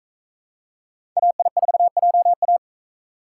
Morse code, heard as a keyed tone, about 39 wpm, 700 Hz. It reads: AI4JA